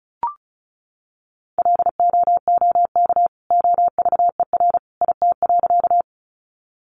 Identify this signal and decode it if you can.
Morse code, heard as a keyed tone, about 34 wpm, 700 Hz. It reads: LOOK OVER IT.
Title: Sentences from Top 100 Words 35wpm